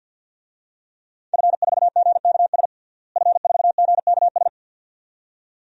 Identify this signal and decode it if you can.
Morse code, heard as a keyed tone, about 46 wpm, 700 Hz. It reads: F4GKS F4GKS
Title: DXCC-100 Call Signs - 5 Characters - Repeated 2x - 50wpm